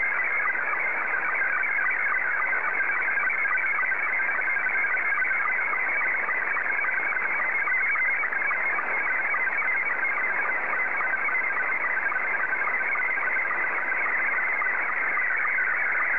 SSTV (slow-scan television), Scottie 2.
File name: sstv / S2-JA3QJQ